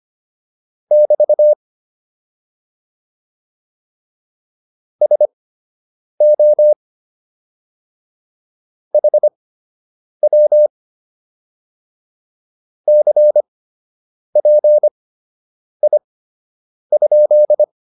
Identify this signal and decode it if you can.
Morse code, heard as a keyed tone, about 25 wpm, spaced as for 8 wpm, 600 Hz. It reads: <BT> SO HW CPI?